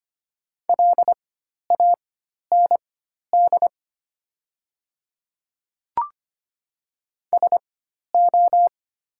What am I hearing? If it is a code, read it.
Morse code, heard as a keyed tone, about 25 wpm, spaced as for 12 wpm, 690 Hz. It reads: LAND SO
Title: 4x Character Spacing - Top 200 Words - 25wpm